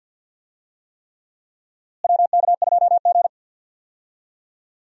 Morse code, heard as a keyed tone, about 52 wpm, 700 Hz. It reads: WK2G
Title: USA Call Signs - 4 Characters - Repeated 3x - 50wpm